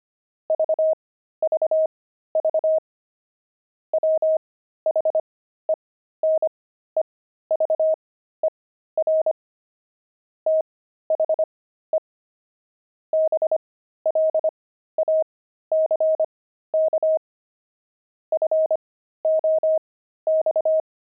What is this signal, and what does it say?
Morse code, heard as a keyed tone, about 25 wpm, spaced as for 13 wpm, 630 Hz. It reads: VVV WHENEVER THE BLACK FOX
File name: pangram-fs-13-cs-25-squirrel